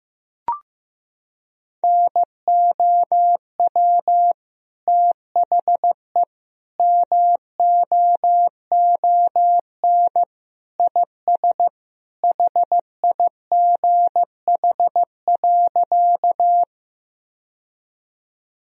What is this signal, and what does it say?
Morse code, heard as a keyed tone, about 15 wpm, 700 Hz. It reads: NOW THE MOON IS HIGH.
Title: Sentences from Top 1000 Words - Review Part 3 - 15wpm